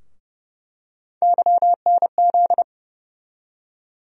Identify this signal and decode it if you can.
Morse code, heard as a keyed tone, about 30 wpm, 700 Hz. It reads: YNZ
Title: USA Call Sign Postfixes - 3 Letters 30wpm